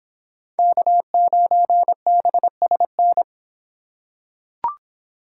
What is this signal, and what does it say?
Morse code, heard as a keyed tone, about 26 wpm, 700 Hz. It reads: K9BSN